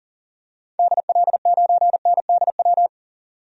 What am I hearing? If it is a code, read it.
Morse code, heard as a keyed tone, about 40 wpm, 700 Hz. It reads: DL9NDW
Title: DXCC-100 Call Signs - 6 Characters - 40wpm